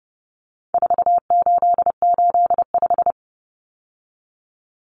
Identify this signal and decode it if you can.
Morse code, heard as a keyed tone, about 30 wpm, 700 Hz. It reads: 4885
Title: Common Cut Numbers - 4-Digit 30wpm